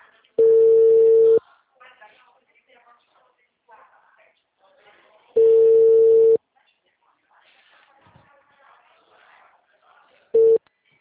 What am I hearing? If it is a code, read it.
Ringback tone (China, 450 Hz), with speech.
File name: out- -2013- - - .65